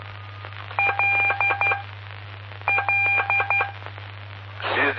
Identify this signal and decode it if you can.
Morse code, heard as a keyed tone, about 12 wpm, spaced as for 6 wpm, 2400 Hz, with noise. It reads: LLE
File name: pp5bar / aulas de telegrafia / Aula-07